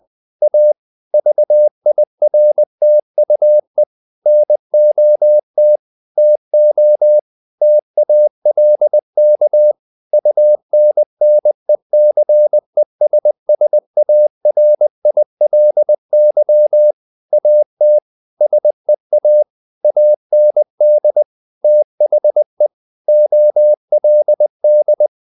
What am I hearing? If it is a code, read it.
Morse code, heard as a keyed tone, about 20 wpm, 600 Hz. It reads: A VIRTUE NOT TO TALK UNNECESSARILY AT SEA AND THE OLD
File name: the old man and the sea / the old man and the sea part 1 20wpm